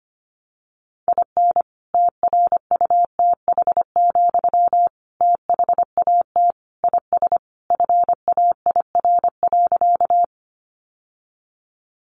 Morse code, heard as a keyed tone, about 25 wpm, 700 Hz. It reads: IN TRUTH, THAT IS FAIR.
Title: Sentences from Top 1300 Words 25wpm